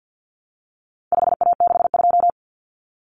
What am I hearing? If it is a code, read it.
Morse code, heard as a keyed tone, about 50 wpm, 700 Hz. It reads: 5A62
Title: Common Cut Numbers - 4-Digit 50wpm